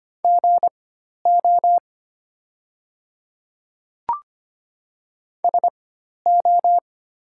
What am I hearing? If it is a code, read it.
Morse code, heard as a keyed tone, about 25 wpm, spaced as for 12 wpm, 700 Hz. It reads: GO SO
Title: 4x Character Spacing - Top 100 Words - 25wpm